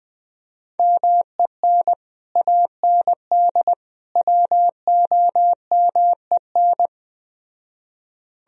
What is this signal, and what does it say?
Morse code, heard as a keyed tone, about 20 wpm, 700 Hz. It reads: MEN AND WOMEN
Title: Speed-Racing - Sets of 3 Words 17wpm